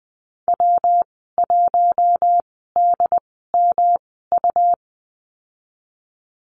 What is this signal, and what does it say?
Morse code, heard as a keyed tone, about 20 wpm, spaced as for 14 wpm, 700 Hz. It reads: W1DMU
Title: 122 Lesson 20g - Number 3 - Call Signs 20wpm